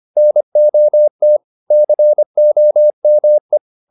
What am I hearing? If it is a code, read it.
Morse code, heard as a keyed tone, about 25 wpm, 600 Hz. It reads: NOT COME